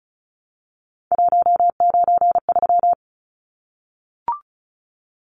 Morse code, heard as a keyed tone, about 34 wpm, 700 Hz. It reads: 193
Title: Numbers - 3-Digit 35wpm